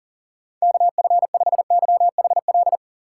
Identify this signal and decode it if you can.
Morse code, heard as a keyed tone, about 40 wpm, 700 Hz. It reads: KF5YHL
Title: USA Call Signs - 6 Characters - 2x Character Spacing - 40wpm